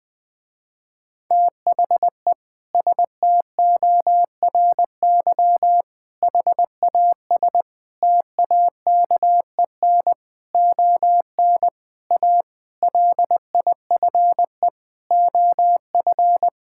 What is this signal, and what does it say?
Morse code, heard as a keyed tone, about 21 wpm, 700 Hz. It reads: THE STORY HAS TAKEN ON A LIFE OF
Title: Sentences with 8-Word Phrases - Spoken Beginning - 20wpm